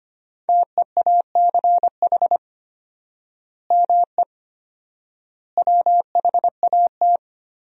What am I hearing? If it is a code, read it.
Morse code, heard as a keyed tone, about 25 wpm, 700 Hz. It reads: TEACH ME WHAT
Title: 4x Word-Spacing - Sentences from Top 500 Words - 25wpm